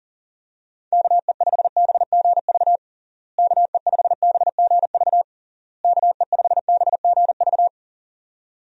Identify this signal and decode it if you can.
Morse code, heard as a keyed tone, about 37 wpm, 700 Hz. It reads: KE5BGV KE5BGV KE5BGV
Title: USA Call Signs - 6 Characters - Repeated 3x - 40wpm